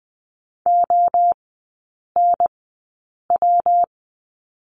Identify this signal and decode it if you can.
Morse code, heard as a keyed tone, about 20 wpm, spaced as for 8 wpm, 700 Hz. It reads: ONW
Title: 145 Lesson 24e - Letter y - Three Characters 20wpm